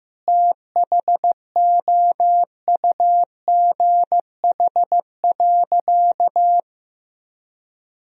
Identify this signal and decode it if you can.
Morse code, heard as a keyed tone, about 15 wpm, 700 Hz. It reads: THOUGH.
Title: Sentences from Top 1000 Words - Review Part 2 - 15wpm